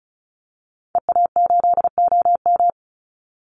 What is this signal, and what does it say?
Morse code, heard as a keyed tone, about 34 wpm, 700 Hz. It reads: EA8OM
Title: Call Signs Repeated 3x - 35wpm